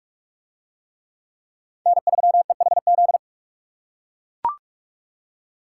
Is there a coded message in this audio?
Yes: Morse code from 1.9 to 3.2 s, heard as a keyed tone, about 41 wpm, 700 Hz. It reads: N3EHZ